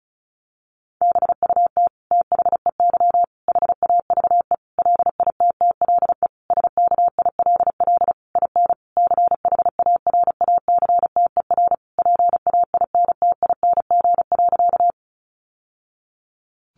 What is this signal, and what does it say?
Morse code, heard as a keyed tone, about 34 wpm, 700 Hz. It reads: BUT THEY HAVE LITTLE SKILL IN CHARACTER PAINTING.
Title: Sentences from Top 1200 Words 35wpm